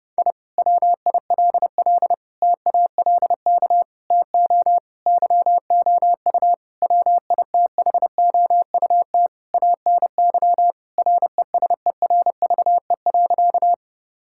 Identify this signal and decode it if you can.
Morse code, heard as a keyed tone, about 30 wpm, 700 Hz. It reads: I WILL TALK TO YOU WITHOUT ANY RESERVE.
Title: Sentences from Top 1000 Words - Review Part 6 - 30wpm